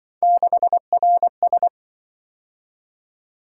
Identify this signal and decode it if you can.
Morse code, heard as a keyed tone, about 25 wpm, 700 Hz. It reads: 6RS
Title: Speed-Racing - Three Letter-Number Combinations - 20wpm